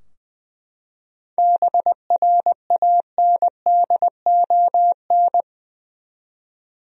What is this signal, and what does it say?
Morse code, heard as a keyed tone, about 20 wpm, 700 Hz. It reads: BRANDON
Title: USA Common Names - 20wpm